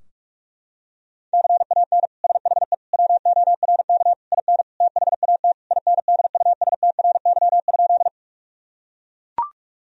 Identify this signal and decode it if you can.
Morse code, heard as a keyed tone, about 41 wpm, 700 Hz. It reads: CAN SHE WORK IN THAT INDUSTRY?
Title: Sentences from Top 1000 Words - Review Part 4 - 45wpm